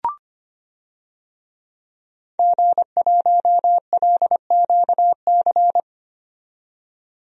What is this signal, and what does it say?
Morse code, heard as a keyed tone, about 25 wpm, 700 Hz. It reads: G1LQC
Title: DXCC-100 Call Signs - 5 Characters - 25wpm